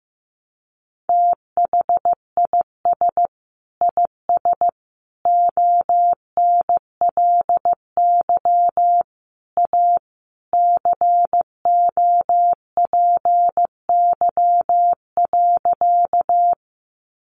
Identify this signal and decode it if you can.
Morse code, heard as a keyed tone, about 15 wpm, 700 Hz. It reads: THIS IS ONLY A COPY.
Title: Sentences from Top 1400 Words 15wpm